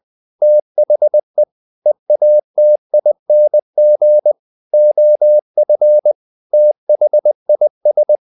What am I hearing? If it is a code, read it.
Morse code, heard as a keyed tone, about 20 wpm, 600 Hz. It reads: THE EATING OF THIS
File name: the old man and the sea part 2 20wpm